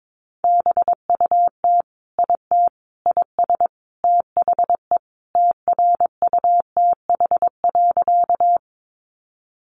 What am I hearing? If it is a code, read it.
Morse code, heard as a keyed tone, about 22 wpm, 700 Hz. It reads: BUT IT IS THE TRUTH.
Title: Sentences from Top 800 Words 22wpm